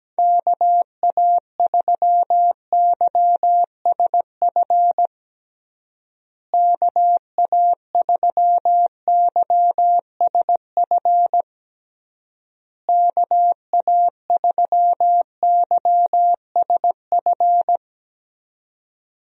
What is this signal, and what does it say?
Morse code, heard as a keyed tone, about 17 wpm, 700 Hz. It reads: KA3YSF KA3YSF KA3YSF
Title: USA Call Signs - 6 Characters - Repeated 3x - 17wpm